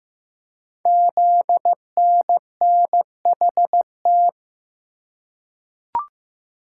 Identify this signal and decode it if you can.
Morse code, heard as a keyed tone, about 15 wpm, 700 Hz. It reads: ZNNHT
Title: Code Groups - 5 Letters - 15wpm